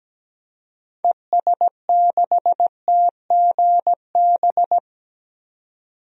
Morse code, heard as a keyed tone, about 17 wpm, 700 Hz. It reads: ES6TGB
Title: DXCC-100 Call Signs - 6 Characters - 17wpm